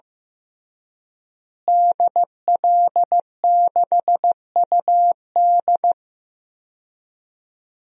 Morse code, heard as a keyed tone, about 15 wpm, 700 Hz. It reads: DL6UD
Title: DXCC-100 Call Signs - 5 Characters - Repeated 3x - 15wpm